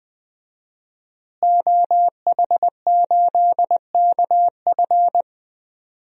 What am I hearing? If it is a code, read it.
Morse code, heard as a keyed tone, about 21 wpm, 700 Hz. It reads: OH8KF
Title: DXCC-100 Call Signs - 5 Characters - Repeated 3x - 20wpm